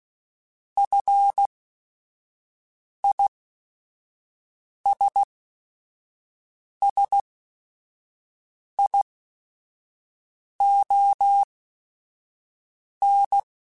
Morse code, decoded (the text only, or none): FISSION